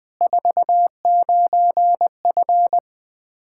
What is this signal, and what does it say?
Morse code, heard as a keyed tone, about 20 wpm, 700 Hz. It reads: 49F
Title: Practice QSO - Exchange 2 - Info - 20wpm